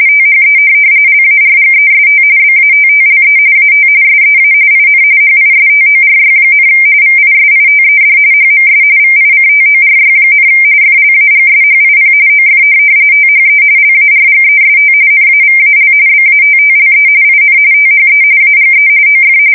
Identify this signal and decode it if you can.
RTTY, about 45.45 baud. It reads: THE LARGEST BATTLE AT SEA IN HISTORY, VISITED HMS VICTORY, FLAGSHIP OF THE LARGEST NAVAL BATTLE IN THE AGE OF
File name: fsk170 / USS-New-Jersey